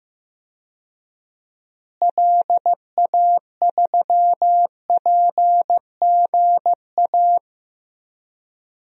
Morse code, heard as a keyed tone, about 15 wpm, 700 Hz. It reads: LA3PGA
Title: DXCC-100 Call Signs - 6 Characters - Repeated 3x - 15wpm